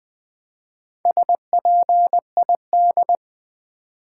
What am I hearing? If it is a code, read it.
Morse code, heard as a keyed tone, about 20 wpm, 700 Hz. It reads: SPID
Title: Build A Word - 2 Letters At A Time - 6-Letter Words - 20wpm